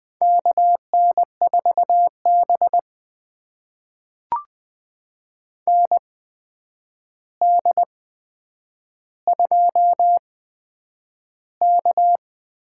Morse code, heard as a keyed tone, about 20 wpm, spaced as for 9 wpm, 700 Hz. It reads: KN4B ND2K
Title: USA Call Signs - 4 Characters - 8x Character Spacing - 20wpm